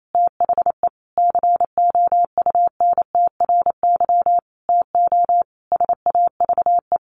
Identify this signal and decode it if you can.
Morse code, heard as a keyed tone, about 28 wpm, 700 Hz. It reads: THE COUNTRY TO SAVE
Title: Sentences from Top 1100 Words 28wpm